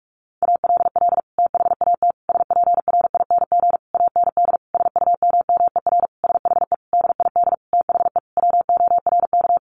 Morse code, heard as a keyed tone, about 44 wpm, 700 Hz. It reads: ALL THAT SPRING AND SUMMER SHE DID THE WORK
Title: Sentences from Top 1200 Words 45wpm